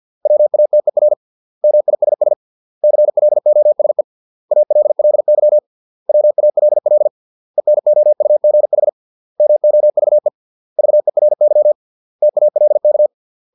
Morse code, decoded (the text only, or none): WATER MISS CLOSE ALLY WALL ENOUGH MOVE VERY TALK